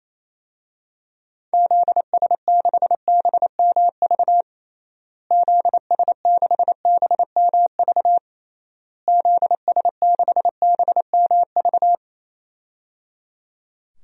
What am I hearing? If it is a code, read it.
Morse code, heard as a keyed tone, about 28 wpm, 700 Hz. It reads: ZS6BMV ZS6BMV ZS6BMV
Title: DXCC-100 Call Signs - 6 Characters - Repeated 3x - 28wpm